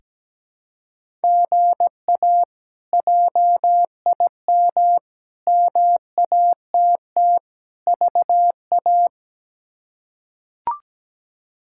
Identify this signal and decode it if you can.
Morse code, heard as a keyed tone, about 17 wpm, 700 Hz. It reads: GA JIM MATT VA
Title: K1USN SST QSO Practice - 17wpm